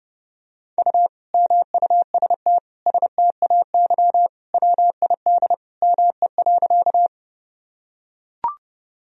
Morse code, heard as a keyed tone, about 30 wpm, 700 Hz. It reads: U MUST STAY WID ME.
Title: Sentences from Top 500 Words and Abbreviations 30wpm